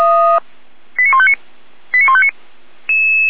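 Five-tone selective call (ZVEI), heard as quick runs of five tones, tones of about 80 ms. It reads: 89159 89159
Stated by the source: ZVEI five-tone call 89168 received by SDR, recorded by fire brigade Feuerwehr Orlamünde.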